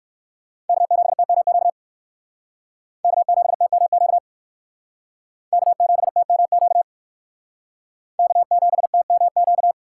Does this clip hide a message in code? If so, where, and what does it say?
Yes: Morse code from 0.7 to 9.7 s, heard as a keyed tone, about 63 wpm, 700 Hz. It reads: K7TMQ K7TMQ K7TMQ K7TMQ